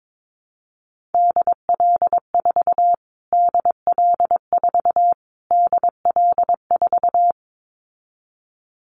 Morse code, heard as a keyed tone, about 22 wpm, 700 Hz. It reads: DL4 DL4 DL4
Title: Callsign Prefixes for Top 25 DXCC - Repeated 3x - 22wpm